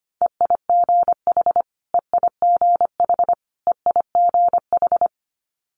Morse code, heard as a keyed tone, about 25 wpm, 700 Hz. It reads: EIGH EIGH EIGH
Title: Multi-letter Phonemes Repeated 3x 25wpm